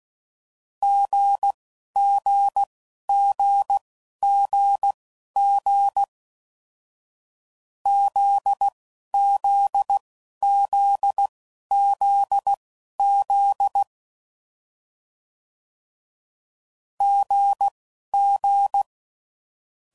Morse code, heard as a keyed tone, about 16 wpm, spaced as for 11 wpm, 800 Hz. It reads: GGGGG ZZZZZ GG
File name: Combinations / Difficult Combos / GZ GZ7